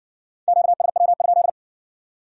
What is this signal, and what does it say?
Morse code, heard as a keyed tone, about 55 wpm, 700 Hz. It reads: QSW?